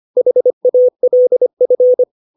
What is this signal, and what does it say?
Morse code, heard as a keyed tone, about 25 wpm, 500 Hz. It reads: HALF
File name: 632 words / 632 Words 25WPM